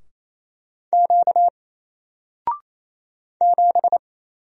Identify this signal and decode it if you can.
Morse code, heard as a keyed tone, about 27 wpm, 700 Hz. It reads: Q 7
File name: Single-Letter-Number-28wpm